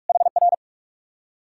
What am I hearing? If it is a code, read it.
Morse code, heard as a keyed tone, about 65 wpm, spaced as for 55 wpm, 700 Hz. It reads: HR
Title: Speed-Racing Common QSO Phrases - 45wpm